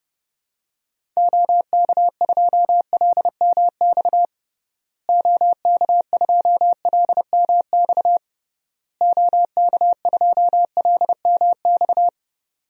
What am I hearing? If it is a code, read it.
Morse code, heard as a keyed tone, about 30 wpm, 700 Hz. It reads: OK2LMX OK2LMX OK2LMX
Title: DXCC-100 Call Signs - 6 Characters - Repeated 3x - 30wpm